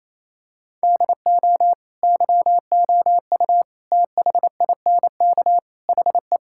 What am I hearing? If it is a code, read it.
Morse code, heard as a keyed tone, about 28 wpm, 700 Hz. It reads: DO YOU THINK HE
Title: Sentences from Top 1600 Words 28wpm